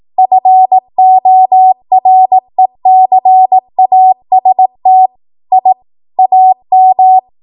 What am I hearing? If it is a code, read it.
Morse code, heard as a keyed tone, about 18 wpm, 750 Hz. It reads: FORECAST I AM
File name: Archive / 18 WPM / 18WPM